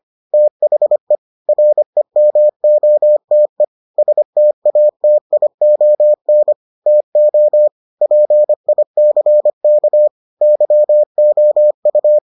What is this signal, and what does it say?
Morse code, heard as a keyed tone, about 25 wpm, 600 Hz. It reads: THE REMOTE STATION TO PICK YOU